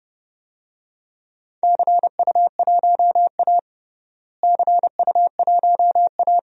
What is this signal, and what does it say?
Morse code, heard as a keyed tone, about 30 wpm, 700 Hz. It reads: CU1A CU1A
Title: DXCC-100 Call Signs - 4 Characters - Repeated 2x - 30wpm